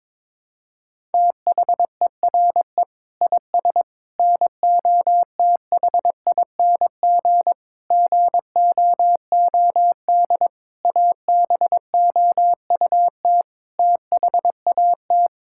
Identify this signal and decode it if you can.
Morse code, heard as a keyed tone, about 22 wpm, 700 Hz. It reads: THERE IS NOTHING GOOD ABOUT THAT
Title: Sentences from Top 1000 Words - Review Part 5 - 22wpm